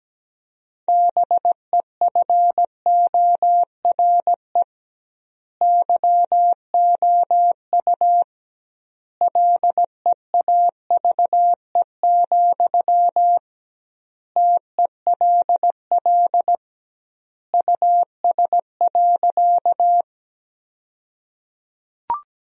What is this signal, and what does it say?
Morse code, heard as a keyed tone, about 17 wpm, 700 Hz. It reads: BEFORE YOU LEAVE, TELL US.
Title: 2x Word-Spacing - Sentences from Top 400 Words - 17wpm